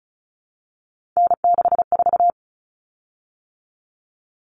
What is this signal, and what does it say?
Morse code, heard as a keyed tone, about 34 wpm, 700 Hz. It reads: N64
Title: Common Cut Numbers - 3-Digit 35wpm